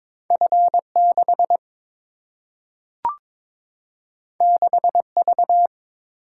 Morse code, heard as a keyed tone, about 22 wpm, 700 Hz. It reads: F6 6V
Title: Single-Letter Single-Number 22wpm